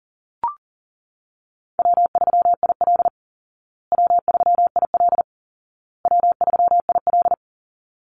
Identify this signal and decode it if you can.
Morse code, heard as a keyed tone, about 40 wpm, 700 Hz. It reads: W3IL W3IL W3IL
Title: Easy Call Signs Repeated 3x - 40wpm